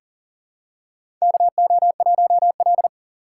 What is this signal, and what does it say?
Morse code, heard as a keyed tone, about 40 wpm, 700 Hz. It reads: KO1L
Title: USA Call Signs - 4 Characters - 4x Character Spacing - 40wpm